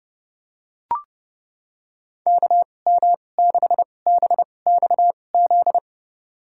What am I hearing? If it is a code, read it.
Morse code, heard as a keyed tone, about 32 wpm, spaced as for 22 wpm, 700 Hz. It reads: KM6BXZ
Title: USA Call Signs - 6 Characters - 2x Character Spacing - 30wpm